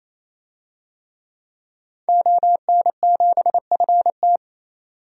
Morse code, heard as a keyed tone, about 28 wpm, 700 Hz. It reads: ON7FT
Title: DXCC-100 Call Signs - 5 Characters - 28wpm